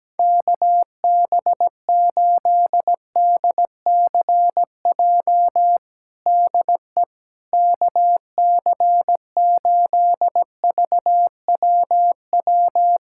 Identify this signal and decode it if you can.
Morse code, heard as a keyed tone, about 17 wpm, 700 Hz. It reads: KB8DCJ DE KC8VWW
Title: USA Callsign DE Callsign - 6 Characters - 17wpm